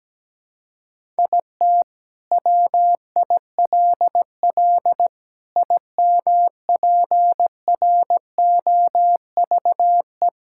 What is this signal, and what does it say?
Morse code, heard as a keyed tone, about 17 wpm, 700 Hz. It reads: IT WILL IMPROVE
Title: Sentences from Top 1000 Words - Review Part 5 - 17wpm